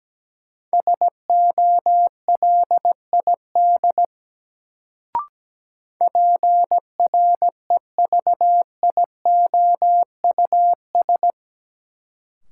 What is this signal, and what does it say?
Morse code, heard as a keyed tone, about 17 wpm, 700 Hz. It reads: SOLID PREVIOUS
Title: Sentences from Top 1500 Words - New Word Drill - 17wpm